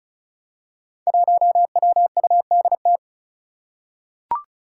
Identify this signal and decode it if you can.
Morse code, heard as a keyed tone, about 34 wpm, 700 Hz. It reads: 1WUDT
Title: Five Challenging Letter-Number Combinations 35wpm